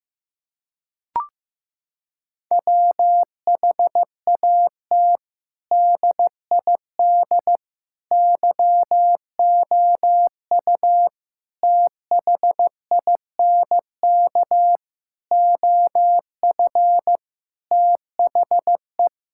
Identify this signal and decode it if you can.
Morse code, heard as a keyed tone, about 15 wpm, 700 Hz. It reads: WHAT DID YOU THINK OF THE
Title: Sentences from Top 1000 Words - Review Part 5 - 15wpm